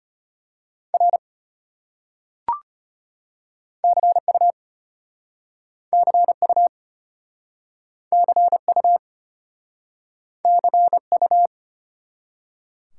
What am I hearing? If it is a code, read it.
Morse code, heard as a keyed tone, about 33 wpm, 700 Hz. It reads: R CU CU CU CU